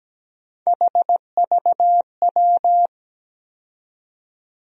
Morse code, heard as a keyed tone, about 17 wpm, 700 Hz. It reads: HVW
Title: USA Call Sign Postfixes - 3 Letters 17wpm